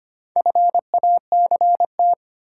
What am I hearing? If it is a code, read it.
Morse code, heard as a keyed tone, about 25 wpm, 700 Hz. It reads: FACT
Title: Sets of 2 Words - New Word Drill - 25wpm